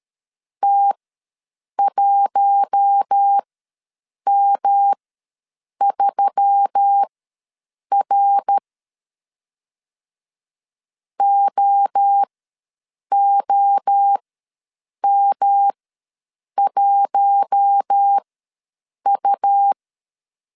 Morse code, heard as a keyed tone, about 13 wpm, spaced as for 7 wpm, 800 Hz. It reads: T1M3R OOM1U
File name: LEZIONI / 5 LBVC / 5 L ws 12wpm 2958